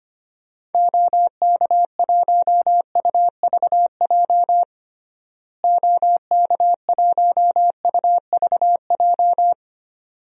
Morse code, heard as a keyed tone, about 25 wpm, 700 Hz. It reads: OK1UVJ OK1UVJ
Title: DXCC-100 Call Signs - 6 Characters - Repeated 2x - 25wpm